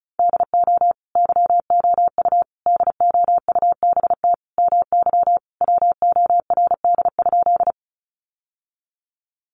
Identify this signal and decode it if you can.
Morse code, heard as a keyed tone, about 34 wpm, 700 Hz. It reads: DO YOU DOUBT MY WORD?
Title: Sentences from Top 1500 Words 35wpm